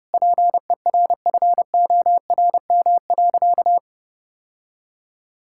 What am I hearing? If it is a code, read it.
Morse code, heard as a keyed tone, about 30 wpm, 700 Hz. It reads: PERFORM.
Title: Sentences from Top 1000 Words - Review Part 5 - 30wpm